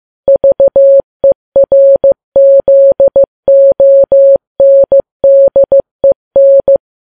Morse code, heard as a keyed tone, about 15 wpm, 550 Hz. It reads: VERZONDEN
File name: Files Web tls / Morse on9fv / morse15kps